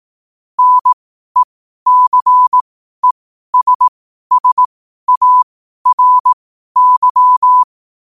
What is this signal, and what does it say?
Morse code, heard as a keyed tone, about 18 wpm, spaced as for 12 wpm, 1000 Hz. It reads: NECESSARY
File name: qotdfiles / 12wpm